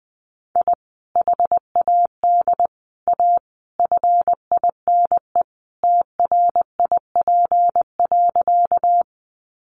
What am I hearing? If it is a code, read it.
Morse code, heard as a keyed tone, about 20 wpm, 700 Hz. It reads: I HAD A FINE TRIP.
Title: Sentences from Top 1300 Words 20wpm